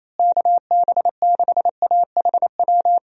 Morse code, heard as a keyed tone, about 27 wpm, 700 Hz. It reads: KB6AHW
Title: USA Call Signs - 6 Characters - Repeated 3x - 28wpm